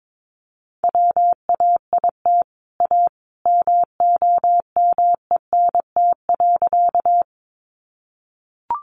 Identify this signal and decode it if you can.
Morse code, heard as a keyed tone, about 22 wpm, 700 Hz. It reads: WAIT A MOMENT.
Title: Sentences from Top 1000 Words 22wpm